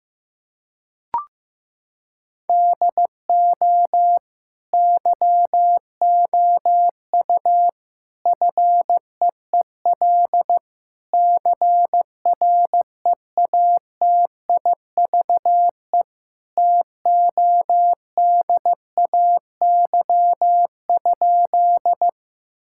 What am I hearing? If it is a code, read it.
Morse code, heard as a keyed tone, about 15 wpm, 700 Hz. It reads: DO YOU FEEL CREATIVE TODAY?
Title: Sentences from Top 1000 Words - Review Part 5 - 15wpm